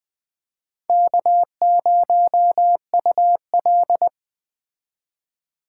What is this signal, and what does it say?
Morse code, heard as a keyed tone, about 20 wpm, 700 Hz. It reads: K0UL
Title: Speed Racing - USA 1x2 Call Signs 17wpm